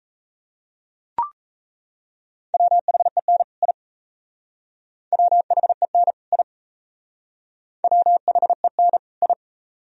Morse code, heard as a keyed tone, about 37 wpm, 700 Hz. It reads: WHEN I WHEN I WHEN I